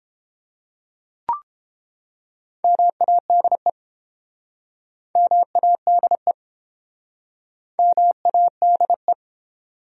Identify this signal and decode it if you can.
Morse code, heard as a keyed tone, about 31 wpm, 700 Hz. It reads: MADE MADE MADE